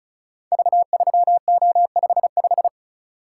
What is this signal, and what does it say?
Morse code, heard as a keyed tone, about 36 wpm, 700 Hz. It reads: V3O55
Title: Code Groups - 5 Letter:Numbers - 35wpm